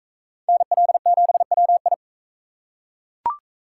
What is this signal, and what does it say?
Morse code, heard as a keyed tone, about 39 wpm, 700 Hz. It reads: NL7WI